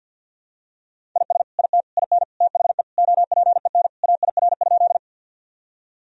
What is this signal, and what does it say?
Morse code, heard as a keyed tone, about 52 wpm, 700 Hz. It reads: IS IT IN THE OPEN AIR?
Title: Sentences from Top 1000 Words - Review Part 1 - 50wpm